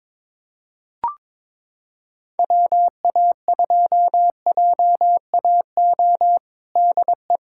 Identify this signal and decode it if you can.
Morse code, heard as a keyed tone, about 22 wpm, 700 Hz. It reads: WA2JAO DE
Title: USA Callsign DE Callsign 22wpm